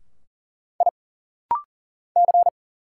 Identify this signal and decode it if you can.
Morse code, heard as a keyed tone, about 40 wpm, 700 Hz. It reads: I C
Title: Single Letter-Number Alt - Rapid-Fire 40wpm